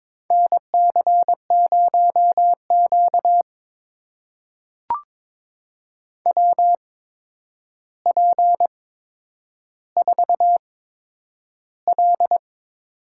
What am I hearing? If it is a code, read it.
Morse code, heard as a keyed tone, about 22 wpm, spaced as for 9 wpm, 700 Hz. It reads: NC0Q WP4L
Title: USA Call Signs - 4 Characters - 8x Character Spacing - 22wpm